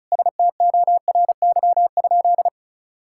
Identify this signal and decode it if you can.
Morse code, heard as a keyed tone, about 34 wpm, 700 Hz. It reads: STORY?
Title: Sentences from Top 1000 Words - Review Part 5 - 35wpm